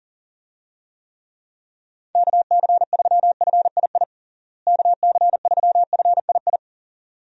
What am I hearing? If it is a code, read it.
Morse code, heard as a keyed tone, about 40 wpm, 700 Hz. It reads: KC3FII KC3FII
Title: USA Call Signs - 6 Characters - Repeated 2x - 40wpm